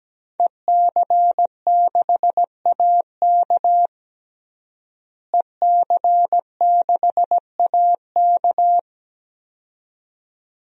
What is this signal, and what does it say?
Morse code, heard as a keyed tone, about 17 wpm, 700 Hz. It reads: EC6AK EC6AK
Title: DXCC-100 Call Signs - 5 Characters - Repeated 2x - 17wpm